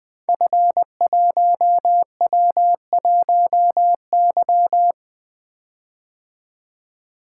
Morse code, heard as a keyed tone, about 20 wpm, 700 Hz. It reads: F1W1Y
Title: Code Groups - 5 Letter:Numbers - 20wpm